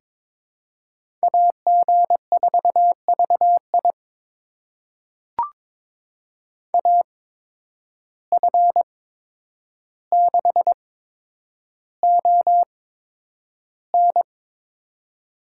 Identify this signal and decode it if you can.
Morse code, heard as a keyed tone, about 23 wpm, spaced as for 10 wpm, 700 Hz. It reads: AG4VI AF6ON
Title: USA Call Signs - 5 Characters - 8x Character Spacing - 22wpm